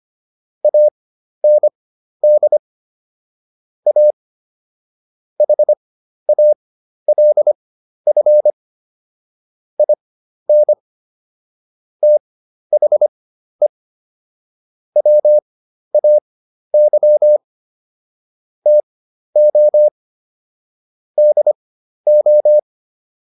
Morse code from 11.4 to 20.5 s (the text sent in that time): THE WAY TO